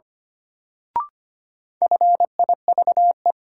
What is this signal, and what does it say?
Morse code, heard as a keyed tone, about 25 wpm, 700 Hz. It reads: FIVE
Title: Sets of 5 Words - New Word Drill - 25wpm